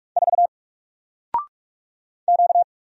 Morse code, heard as a keyed tone, about 46 wpm, 700 Hz. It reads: 4 Q
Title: Single Letter-Number Alt 45wpm